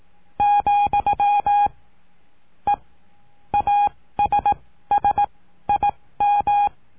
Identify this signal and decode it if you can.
Morse code, heard as a keyed tone, about 18 wpm, spaced as for 14 wpm, 820 Hz. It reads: , E ASSIM
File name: cursos / CURSO DE TELEGRAFIA / Telegrafia - Trilha 19